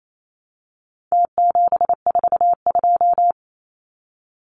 Morse code, heard as a keyed tone, about 28 wpm, 700 Hz. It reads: T742
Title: Numbers - Contest Style - 4-Digit 28wpm